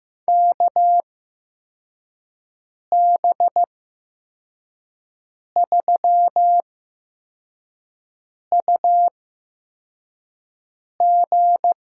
Morse code, heard as a keyed tone, about 15 wpm, spaced as for 4 wpm, 700 Hz. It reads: KB3UG
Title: USA Call Signs - 8x Character Spacing - 15wpm